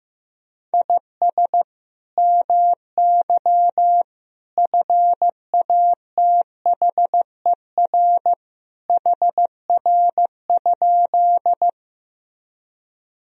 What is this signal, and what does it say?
Morse code, heard as a keyed tone, about 15 wpm, 700 Hz. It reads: IS MY FATHER HR?
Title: Sentences from Top 300 Words and Abbreviations 15wpm